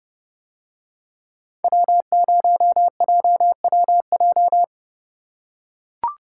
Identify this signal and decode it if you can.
Morse code, heard as a keyed tone, about 29 wpm, 700 Hz. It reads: W0JWJ